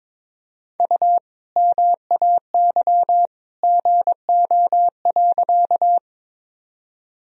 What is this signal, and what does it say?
Morse code, heard as a keyed tone, about 21 wpm, 700 Hz. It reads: U MAY GO.
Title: Sentences from Top 100 Words and Abbreviations 22wpm